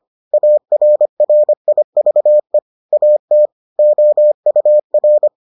Morse code, heard as a keyed tone, about 25 wpm, 600 Hz. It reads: ARRIVE AT OUR